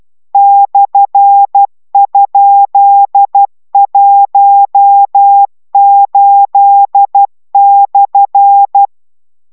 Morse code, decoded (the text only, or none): /?18/